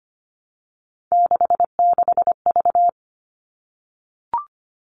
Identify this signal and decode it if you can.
Morse code, heard as a keyed tone, about 25 wpm, 700 Hz. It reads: 66V
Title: All Cut Numbers - 3-Digit 25wpm